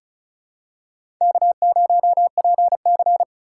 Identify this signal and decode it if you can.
Morse code, heard as a keyed tone, about 36 wpm, 700 Hz. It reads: K0PC
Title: K1USN SST QSO Practice - 35wpm